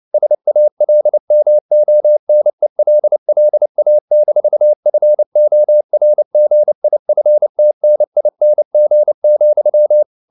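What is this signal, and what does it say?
Morse code, heard as a keyed tone, about 29 wpm, 600 Hz. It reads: SALMONELLA-FORGIFTNING,
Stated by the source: Fast (QRQ) Morse at 29 wpm, from CW med Gnister.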